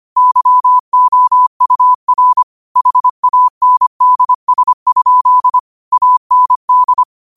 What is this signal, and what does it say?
Morse code, heard as a keyed tone, about 25 wpm, 1000 Hz. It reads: YOUR HANDS? AND
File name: qotdfiles / 25wpm